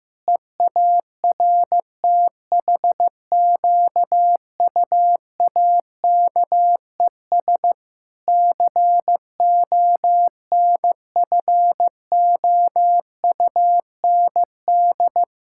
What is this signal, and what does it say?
Morse code, heard as a keyed tone, about 14 wpm, 700 Hz. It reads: EARTHQUAKES CONFOUND
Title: Pangram Challenge v2 - 15wpm